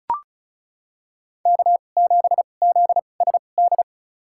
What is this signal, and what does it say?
Morse code, heard as a keyed tone, about 36 wpm, spaced as for 25 wpm, 700 Hz. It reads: K7ZSD
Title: USA Call Signs - 5 Characters - 2x Character Spacing - 35wpm